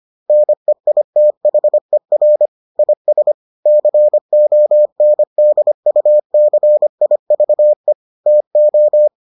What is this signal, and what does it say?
Morse code, heard as a keyed tone, about 25 wpm, 600 Hz. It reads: NEITHER IS CONDUCIVE TO